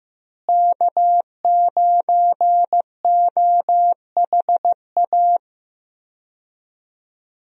Morse code, heard as a keyed tone, about 15 wpm, 700 Hz. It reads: K9OHA
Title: USA Call Signs - 5 Characters - Repeated 3x - 15wpm